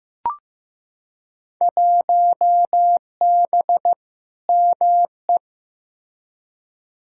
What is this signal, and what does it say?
Morse code, heard as a keyed tone, about 15 wpm, 700 Hz. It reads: 1B ME
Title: Field Day Exchanges - 15wpm